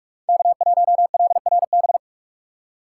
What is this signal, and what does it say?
Morse code, heard as a keyed tone, about 44 wpm, 700 Hz. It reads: K1LRB
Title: USA Call Signs - 5 Characters - 3x Character Spacing - 45wpm